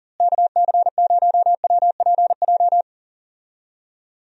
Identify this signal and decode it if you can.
Morse code, heard as a keyed tone, about 37 wpm, 700 Hz. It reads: KC0WPJ
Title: USA Call Signs - 6 Characters - Repeated 2x - 40wpm